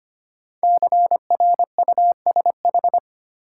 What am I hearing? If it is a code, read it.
Morse code, heard as a keyed tone, about 25 wpm, 700 Hz. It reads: CRUSH
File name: 5 Letter Words - First 3 Letters Given - 25wpm